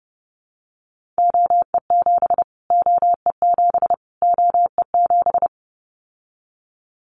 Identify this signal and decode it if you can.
Morse code, heard as a keyed tone, about 30 wpm, 700 Hz. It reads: OE7 OE7 OE7
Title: Callsign Prefixes for Top 25 DXCC - Repeated 3x - 30wpm